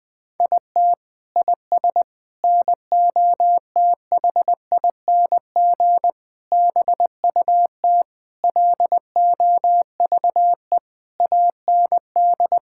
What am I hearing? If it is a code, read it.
Morse code, heard as a keyed tone, about 21 wpm, 700 Hz. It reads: IT IS NOTHING BUT LOVE AND
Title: Sentences from Top 1600 Words 20wpm